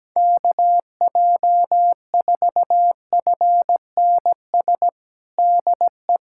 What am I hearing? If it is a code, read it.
Morse code, heard as a keyed tone, about 17 wpm, 700 Hz. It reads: KJ4FNS DE
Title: USA Callsign DE Callsign 17wpm